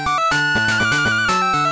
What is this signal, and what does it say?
There is no code.